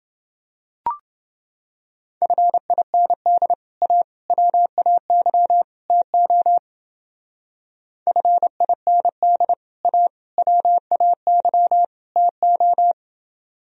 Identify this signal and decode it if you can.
Morse code, heard as a keyed tone, about 28 wpm, 700 Hz. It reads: FIND A WAY TO FIND A WAY TO